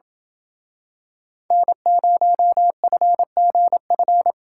Morse code, heard as a keyed tone, about 27 wpm, 700 Hz. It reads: N0FGF